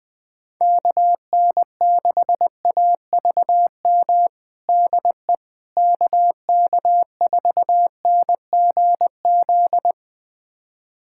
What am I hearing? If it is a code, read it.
Morse code, heard as a keyed tone, about 20 wpm, 700 Hz. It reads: KN6AVM DE KK4NGZ
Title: USA Callsign DE Callsign 20wpm